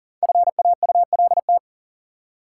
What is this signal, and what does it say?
Morse code, heard as a keyed tone, about 37 wpm, 700 Hz. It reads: FAULT
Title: Five Letter Words - 40wpm